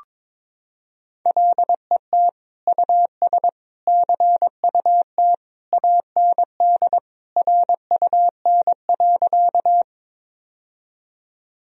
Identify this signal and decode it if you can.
Morse code, heard as a keyed tone, about 22 wpm, 700 Hz. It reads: LET US CUT AND RUN.
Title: Sentences from Top 1000 Words - Review Part 1 - 22wpm